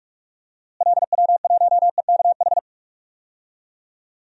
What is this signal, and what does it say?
Morse code, heard as a keyed tone, about 41 wpm, 690 Hz. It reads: LW1EKH
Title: DXCC-100 Call Signs - 6 Characters - Repeated 2x - 45wpm